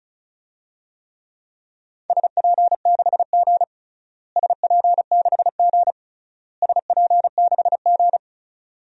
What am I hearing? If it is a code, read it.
Morse code, heard as a keyed tone, about 36 wpm, 700 Hz. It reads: SP6G SP6G SP6G
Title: DXCC-100 Call Signs - 4 Characters - Repeated 3x - 35wpm